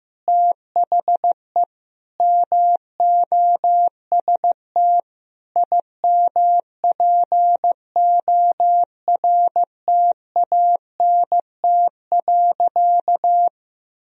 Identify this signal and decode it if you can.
Morse code, heard as a keyed tone, about 15 wpm, 700 Hz. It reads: THE MOST IMPORTANT.
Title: Sentences from Top 1000 Words - Review Part 2 - 15wpm